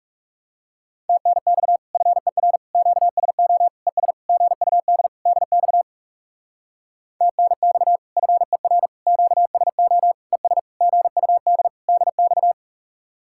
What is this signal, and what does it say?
Morse code, heard as a keyed tone, about 39 wpm, 700 Hz. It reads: TNX FER QSO ES GUD DX TNX FER QSO ES GUD DX